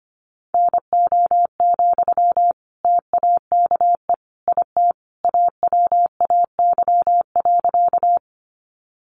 Morse code, heard as a keyed tone, about 25 wpm, 700 Hz. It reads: NO, TAKE IT AWAY.
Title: Sentences from Top 600 Words 25wpm